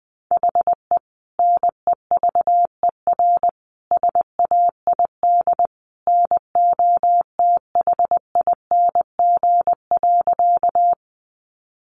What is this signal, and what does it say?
Morse code, heard as a keyed tone, about 20 wpm, 700 Hz. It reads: HE NEVER SAID NOTHING.
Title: Sentences from Top 500 Words 20wpm